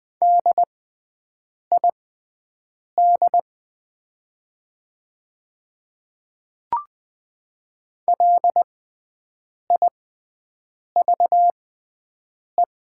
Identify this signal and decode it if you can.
Morse code, heard as a keyed tone, about 20 wpm, spaced as for 7 wpm, 700 Hz. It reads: DID LIVE
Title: 6x Character Spacing - Top 300 Words - 20wpm